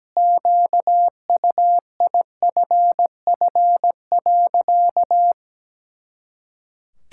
Morse code, heard as a keyed tone, about 17 wpm, 700 Hz. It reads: QUIFF.
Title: Pangram Challenge v2 - 17wpm